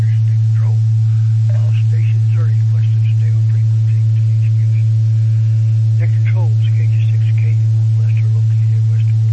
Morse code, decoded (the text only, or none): none